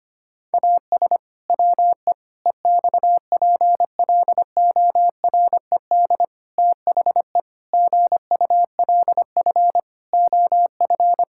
Morse code, heard as a keyed tone, about 25 wpm, 700 Hz. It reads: AS WE EXPLORED THE GULF OF
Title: Pangram Challenge v2 - 25wpm